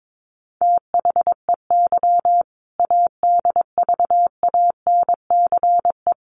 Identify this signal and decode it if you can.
Morse code, heard as a keyed tone, about 22 wpm, 700 Hz. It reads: THEY ADVANCE
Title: Sentences from Top 1400 Words 22wpm